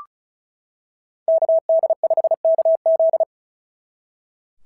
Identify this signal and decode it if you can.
Morse code, heard as a keyed tone, about 34 wpm, 690 Hz. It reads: KD5KZ
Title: SOTA QSO Practice 35wpm